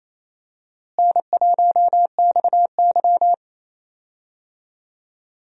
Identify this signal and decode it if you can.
Morse code, heard as a keyed tone, about 28 wpm, 700 Hz. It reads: N1XY
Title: USA Call Signs - 4 Characters - Repeated 3x - 28wpm